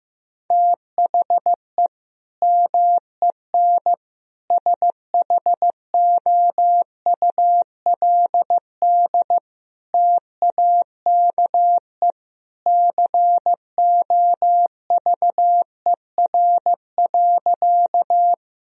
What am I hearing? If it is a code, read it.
Morse code, heard as a keyed tone, about 16 wpm, 700 Hz. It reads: THE MEN SHOULD TAKE COVER.
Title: Sentences from Top 1000 Words - Review Part 1 - 15wpm